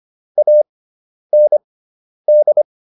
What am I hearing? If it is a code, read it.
Morse code, heard as a keyed tone, about 25 wpm, spaced as for 10 wpm, 600 Hz. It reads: AND